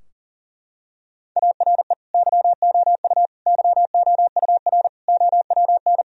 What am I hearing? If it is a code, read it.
Morse code, heard as a keyed tone, about 40 wpm, 700 Hz. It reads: ARE YOU YOUR OWN
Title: Sentences from Top 1000 Words - Review Part 4 - 40wpm